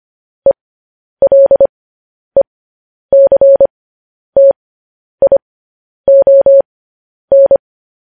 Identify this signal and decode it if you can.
Morse code, heard as a keyed tone, about 25 wpm, spaced as for 10 wpm, 550 Hz. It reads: ELECTION
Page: news-headlines-in-morse-code-at-10-wpm